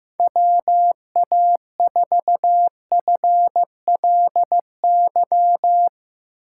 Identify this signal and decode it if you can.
Morse code, heard as a keyed tone, about 15 wpm, 700 Hz. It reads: WA4FLY
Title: USA Call Signs - 6 Characters - 3x Character Spacing - 15wpm